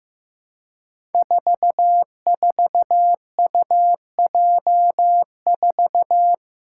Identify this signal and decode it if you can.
Morse code, heard as a keyed tone, about 15 wpm, 700 Hz. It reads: 44UJ4
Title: Five Challenging Letter-Number Combinations 15wpm